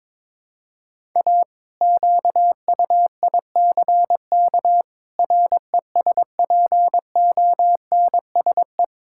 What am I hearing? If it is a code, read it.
Morse code, heard as a keyed tone, about 22 wpm, 700 Hz. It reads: A QUICK RESPONSE
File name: practice / 22wpm / Sentences with 8-Word Phrases - 22wpm